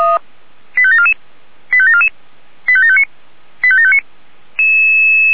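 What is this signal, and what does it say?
Five-tone selective call (ZVEI), heard as quick runs of five tones, tones of about 80 ms. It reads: 85744 85744 85759 85759
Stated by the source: ZVEI five-tone call 85725 received by SDR.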